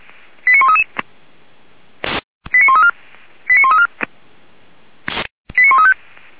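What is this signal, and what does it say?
Five-tone selective call (ZVEI), heard as quick runs of five tones, tones of about 80 ms. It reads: 89133 89134 89134 89135
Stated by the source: ZVEI five-tone call 89132 received by SDR, logged by Feuerwehr Orlamünde.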